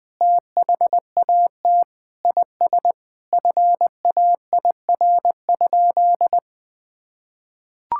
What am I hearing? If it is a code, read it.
Morse code, heard as a keyed tone, about 20 wpm, 700 Hz. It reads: THAT IS FAIR?
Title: Sentences with 4-Word Phrases - 20wpm